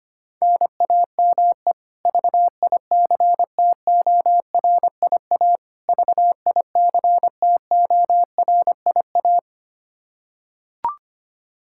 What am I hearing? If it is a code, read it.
Morse code, heard as a keyed tone, about 25 wpm, 700 Hz. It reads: NAME VICTORIA VICTORIA
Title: K1USN SST QSO Practice 25wpm